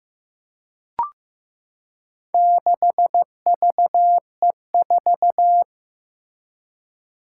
Morse code, heard as a keyed tone, about 15 wpm, 700 Hz. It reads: 6VE4
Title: Four Challenging Letter-Number Combinations 15wpm